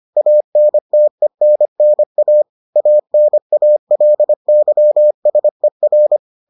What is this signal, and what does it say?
Morse code, heard as a keyed tone, about 25 wpm, 600 Hz. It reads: ANTENNA ANALYSER